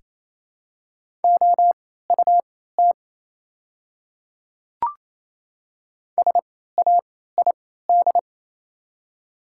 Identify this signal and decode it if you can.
Morse code, heard as a keyed tone, about 28 wpm, spaced as for 16 wpm, 700 Hz. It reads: OUT SAID
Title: 3x Character Spacing - Top 100 Words - 28wpm